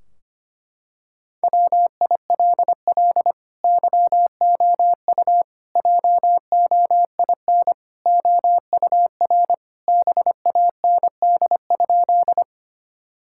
Morse code, heard as a keyed tone, about 25 wpm, 700 Hz. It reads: WILL YOU JOIN OUR BAND?
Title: Sentences from Top 1600 Words 25wpm